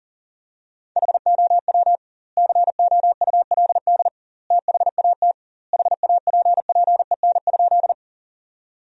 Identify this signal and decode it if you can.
Morse code, heard as a keyed tone, about 40 wpm, 700 Hz. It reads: HOW COULD THAT HAPPEN?
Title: 2x Word-Spacing - Sentences from Top 400 Words - 40wpm